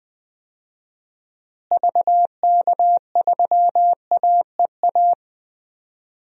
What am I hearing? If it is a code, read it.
Morse code, heard as a keyed tone, about 20 wpm, 700 Hz. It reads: VK3AEA
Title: DXCC-100 Call Signs - 6 Characters - Repeated 2x - 20wpm